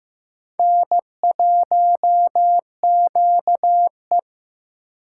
Morse code, heard as a keyed tone, about 18 wpm, 700 Hz. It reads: N1QE
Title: Speed Racing - USA 1x2 Call Signs 15wpm